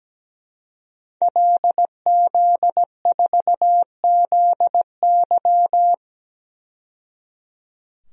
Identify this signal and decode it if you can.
Morse code, heard as a keyed tone, about 17 wpm, 700 Hz. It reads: LZ4ZY
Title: DXCC-100 Call Signs - 5 Characters - 17wpm